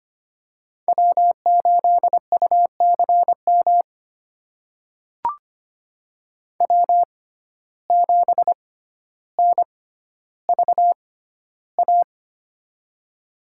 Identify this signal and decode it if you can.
Morse code, heard as a keyed tone, about 25 wpm, spaced as for 13 wpm, 700 Hz. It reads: W8UCM W7NVA
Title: USA Call Signs - 5 Characters - 6x Character Spacing - 25wpm